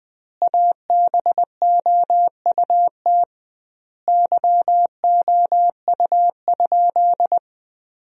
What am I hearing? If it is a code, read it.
Morse code, heard as a keyed tone, about 20 wpm, 700 Hz. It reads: ABOUT YOU?
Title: 2x Word-Spacing - Sentences from Top 100 Words - 20wpm